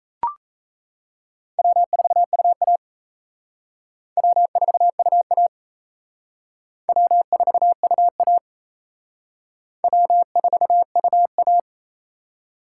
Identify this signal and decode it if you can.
Morse code, heard as a keyed tone, about 34 wpm, 700 Hz. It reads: W4UA W4UA W4UA W4UA
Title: Speed Racing - USA 1x2 Call Signs 28wpm